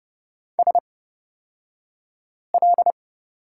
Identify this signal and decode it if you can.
Morse code, heard as a keyed tone, about 30 wpm, spaced as for 5 wpm, 700 Hz. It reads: SL
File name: Lesson 37d - Letter x - Two Characters 30wpm